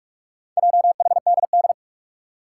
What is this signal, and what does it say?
Morse code, heard as a keyed tone, about 44 wpm, 700 Hz. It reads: JHDD
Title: Four Challenging Letter-Number Combinations 45wpm